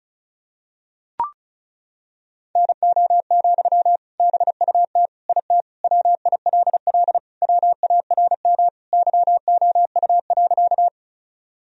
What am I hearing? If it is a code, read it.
Morse code, heard as a keyed tone, about 34 wpm, 700 Hz. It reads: NO, BUT IT WILL WARM YOU.
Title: Sentences from Top 1000 Words - Review Part 2 - 35wpm